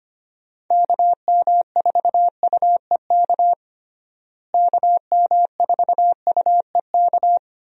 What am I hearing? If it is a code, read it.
Morse code, heard as a keyed tone, about 25 wpm, 700 Hz. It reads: KM4UEK KM4UEK
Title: USA Call Signs - 6 Characters - Repeated 3x - 25wpm